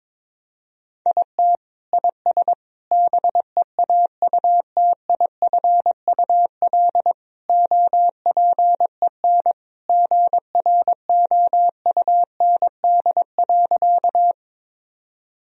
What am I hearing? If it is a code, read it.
Morse code, heard as a keyed tone, about 22 wpm, 700 Hz. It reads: IT IS BEAUTIFUL OPEN GROUND.
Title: Sentences from Top 1000 Words - Review Part 2 - 22wpm